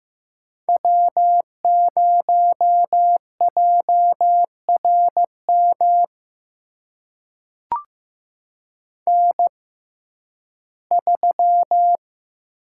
Morse code, heard as a keyed tone, about 15 wpm, spaced as for 11 wpm, 700 Hz. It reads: W0JRM N3
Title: USA Call Signs - 5 Characters - 6x Character Spacing - 15wpm